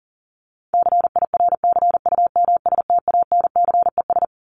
Morse code, heard as a keyed tone, about 40 wpm, 700 Hz. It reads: CIRCUMSTANCES
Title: ES:S Suffix - Top 100 Words with Base Words 40wpm